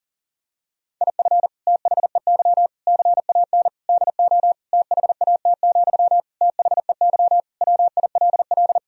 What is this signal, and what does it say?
Morse code, heard as a keyed tone, about 40 wpm, 700 Hz. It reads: IF THEY CAN DO THAT, THEY WILL
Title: Sentences from Top 1000 Words - Review Part 4 - 40wpm